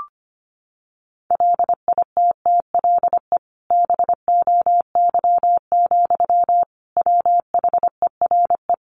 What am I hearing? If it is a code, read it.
Morse code, heard as a keyed tone, about 25 wpm, 700 Hz. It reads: LITTLE BOY, WHERE
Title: Sentences from Top 1000 Words 25wpm